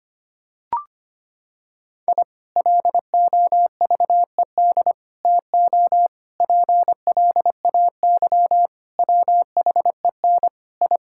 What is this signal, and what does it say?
Morse code, heard as a keyed tone, about 25 wpm, 700 Hz. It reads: I LOVED TO PLAY WHEN I
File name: practice / 25wpm / Sentences with 4-Word Phrases - 25wpm